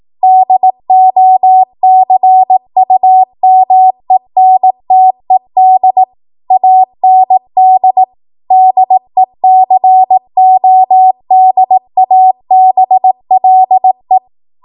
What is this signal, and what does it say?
Morse code, heard as a keyed tone, about 18 wpm, 750 Hz. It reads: DOCUMENTED AND DECODABLE
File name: Morse / 18WPM